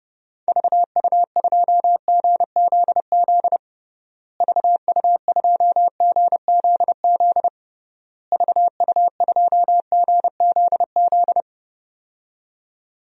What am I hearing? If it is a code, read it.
Morse code, heard as a keyed tone, about 30 wpm, 700 Hz. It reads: VU2GZZ VU2GZZ VU2GZZ
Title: DXCC-100 Call Signs - 6 Characters - Repeated 3x - 30wpm